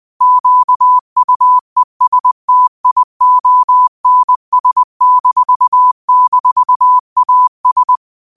Morse code, heard as a keyed tone, about 20 wpm, 1000 Hz. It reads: QUESTIONS--AS
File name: qotdfiles / 20wpm